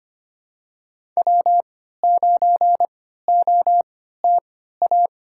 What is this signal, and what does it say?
Morse code, heard as a keyed tone, about 25 wpm, spaced as for 14 wpm, 700 Hz. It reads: W9OTA
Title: USA Call Signs - 5 Characters - 3x Character Spacing - 25wpm